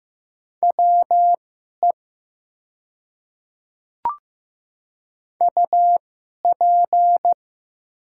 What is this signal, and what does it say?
Morse code, heard as a keyed tone, about 15 wpm, spaced as for 11 wpm, 700 Hz. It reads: WE UP
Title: 2x Character Spacing - Top 100 Words - 15wpm